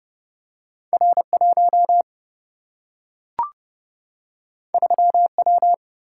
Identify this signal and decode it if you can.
Morse code, heard as a keyed tone, about 30 wpm, 700 Hz. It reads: R1 3W
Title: Single-Letter Single-Number 30wpm